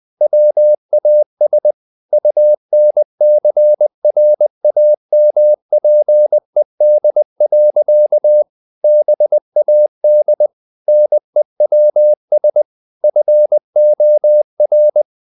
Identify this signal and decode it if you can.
Morse code, heard as a keyed tone, about 20 wpm, 600 Hz. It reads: WAS UNCRAMPED. BAD NEWS FOR
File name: the old man and the sea / the old man and the sea part 1 20wpm